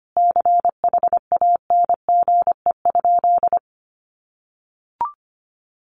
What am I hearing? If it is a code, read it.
Morse code, heard as a keyed tone, about 25 wpm, 700 Hz. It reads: CHANGE?
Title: Sentences from Top 1500 Words 25wpm